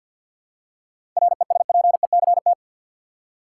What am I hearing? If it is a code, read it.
Morse code, heard as a keyed tone, about 51 wpm, 700 Hz. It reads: RESPECT
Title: Seven Letter Words - 50wpm